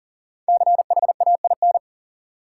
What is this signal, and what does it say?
Morse code, heard as a keyed tone, about 40 wpm, 700 Hz. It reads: CHAIN
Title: Sentences from Top 1500 Words - New Word Drill Alt - 40wpm